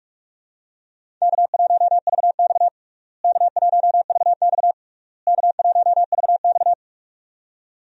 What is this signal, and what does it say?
Morse code, heard as a keyed tone, about 42 wpm, 700 Hz. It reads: K1VX K1VX K1VX
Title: USA Call Signs - 4 Characters - Repeated 3x - 45wpm